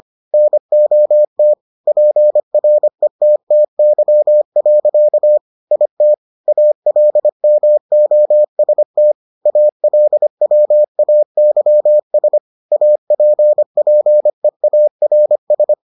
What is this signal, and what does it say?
Morse code, heard as a keyed tone, about 25 wpm, 600 Hz. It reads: NOT PRETTY. IT ALMOST ALWAYS APPEARS